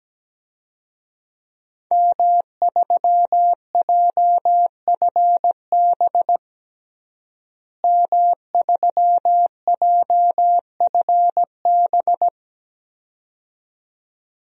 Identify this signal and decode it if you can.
Morse code, heard as a keyed tone, about 17 wpm, 700 Hz. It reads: M3JFB M3JFB
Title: DXCC-100 Call Signs - 5 Characters - Repeated 2x - 17wpm